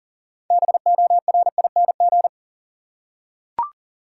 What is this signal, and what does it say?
Morse code, heard as a keyed tone, about 37 wpm, 700 Hz. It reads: BORING
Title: Six Letter Words - 40wpm